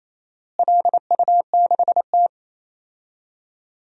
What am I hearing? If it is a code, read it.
Morse code, heard as a keyed tone, about 28 wpm, 700 Hz. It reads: LU6T
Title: DXCC-100 Call Signs - 4 Characters - Repeated 2x - 28wpm